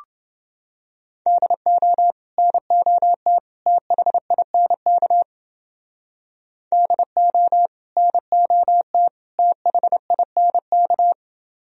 Morse code, heard as a keyed tone, about 29 wpm, 700 Hz. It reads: DO NOT THINK DO NOT THINK